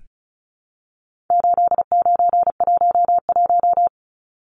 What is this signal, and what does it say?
Morse code, heard as a keyed tone, about 34 wpm, 700 Hz. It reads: 8911
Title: Numbers - 4-Digit 35wpm